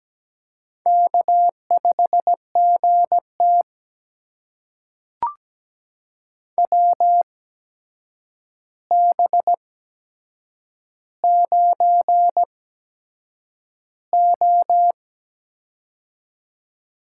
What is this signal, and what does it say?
Morse code, heard as a keyed tone, about 17 wpm, spaced as for 7 wpm, 700 Hz. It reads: K5GT WB9O
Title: USA Call Signs - 4 Characters - 8x Character Spacing - 17wpm